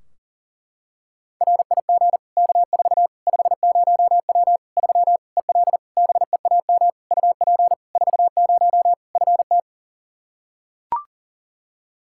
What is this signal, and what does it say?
Morse code, heard as a keyed tone, about 40 wpm, 700 Hz. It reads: RIG K4 50W 3 EL BEAM UP 40 FT
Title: Practice QSO - Exchange 2 - Info - 40wpm